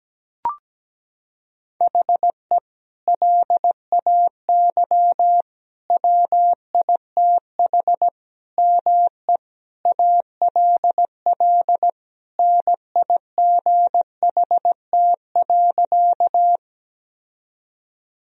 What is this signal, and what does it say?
Morse code, heard as a keyed tone, about 17 wpm, 700 Hz. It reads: HE LAY WITH ME ALL NIGHT.
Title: Sentences from Top 1000 Words - Review Part 2 - 17wpm